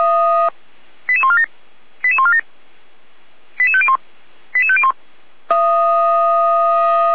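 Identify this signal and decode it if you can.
Five-tone selective call (ZVEI), heard as quick runs of five tones, tones of about 70 ms. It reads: 88147 88147 88501 88501
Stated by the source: ZVEI five-tone call received on SDR.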